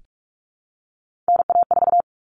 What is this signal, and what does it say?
Morse code, heard as a keyed tone, about 44 wpm, 700 Hz. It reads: NA4
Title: All Cut Numbers - 3-Digit 45wpm